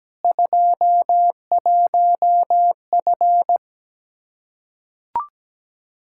Morse code, heard as a keyed tone, about 17 wpm, 700 Hz. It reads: 21F
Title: Practice QSO - Exchange 2 - Info - 17wpm